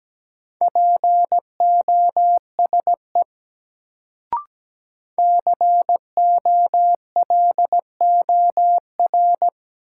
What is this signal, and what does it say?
Morse code, heard as a keyed tone, about 17 wpm, 700 Hz. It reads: POSE COLOR
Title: Sentences from Top 1000 Words - Review Part 2 - New Word Drill - 17wpm